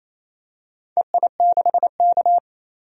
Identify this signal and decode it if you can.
Morse code, heard as a keyed tone, about 28 wpm, 700 Hz. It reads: EI6K
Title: DXCC-100 Call Signs - 4 Characters - 28wpm